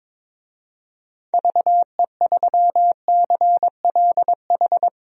Morse code, heard as a keyed tone, about 21 wpm, 700 Hz. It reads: VE3CLH
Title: DXCC-100 Call Signs - 6 Characters - 22wpm